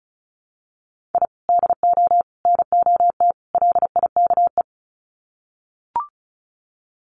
Morse code, heard as a keyed tone, about 34 wpm, 700 Hz. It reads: I DO NOT LIKE
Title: Sets of 4 Words 35wpm